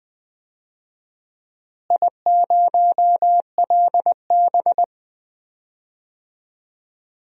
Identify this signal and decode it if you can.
Morse code, heard as a keyed tone, about 20 wpm, 700 Hz. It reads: I0LB
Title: DXCC-100 Call Signs - 4 Characters - Repeated 3x - 20wpm